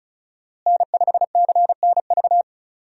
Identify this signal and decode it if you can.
Morse code, heard as a keyed tone, about 34 wpm, 700 Hz. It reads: N5CNV
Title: USA Call Signs - 5 Characters - Repeated 3x - 35wpm